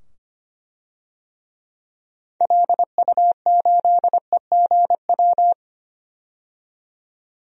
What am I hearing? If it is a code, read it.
Morse code, heard as a keyed tone, about 25 wpm, 700 Hz. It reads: LU8EGW
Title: DXCC-100 Call Signs - 6 Characters - 25wpm